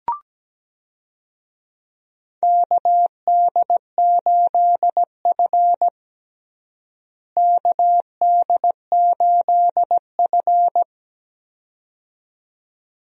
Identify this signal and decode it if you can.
Morse code, heard as a keyed tone, about 17 wpm, 700 Hz. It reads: KD8F KD8F
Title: USA Call Signs - 4 Characters - Repeated 2x - 17wpm